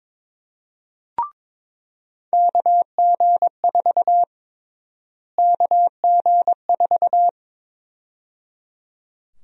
Morse code, heard as a keyed tone, about 22 wpm, 700 Hz. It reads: KG4 KG4
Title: USA Call Sign Prefixes - Repeated 2x - 22wpm